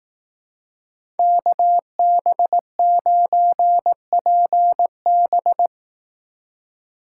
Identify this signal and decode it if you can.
Morse code, heard as a keyed tone, about 18 wpm, 700 Hz. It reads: KB9PB
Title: Speed Racing - USA 2x2 Call Signs 15wpm